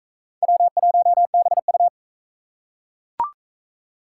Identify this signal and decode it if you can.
Morse code, heard as a keyed tone, about 38 wpm, 700 Hz. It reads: W1BU